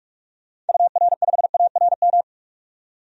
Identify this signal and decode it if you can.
Morse code, heard as a keyed tone, about 46 wpm, 700 Hz. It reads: UR5ARM
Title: DXCC-100 Call Signs - 6 Characters - 45wpm